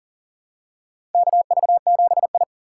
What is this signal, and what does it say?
Morse code, heard as a keyed tone, about 40 wpm, 700 Hz. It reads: KV7I
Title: USA Call Signs - 3x Character Spacing - 40wpm